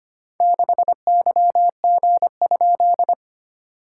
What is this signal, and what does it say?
Morse code, heard as a keyed tone, about 25 wpm, 700 Hz. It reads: 6YG?
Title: SOTA QSO Practice 25wpm